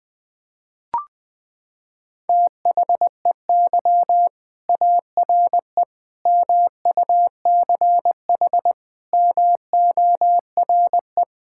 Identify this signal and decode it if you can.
Morse code, heard as a keyed tone, about 20 wpm, 700 Hz. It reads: THEY ARE MUCH MORE
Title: Sentences from Top 1000 Words - Review Part 5 - 20wpm